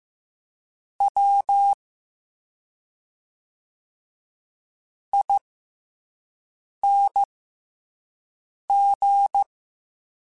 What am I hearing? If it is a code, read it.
Morse code, heard as a keyed tone, about 15 wpm, spaced as for 5 wpm, 800 Hz. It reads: W ING